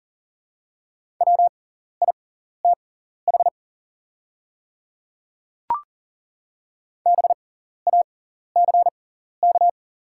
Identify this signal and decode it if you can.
Morse code, heard as a keyed tone, about 40 wpm, spaced as for 14 wpm, 690 Hz. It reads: WITH BACK
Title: 6x Character Spacing - Top 500 Words - 40wpm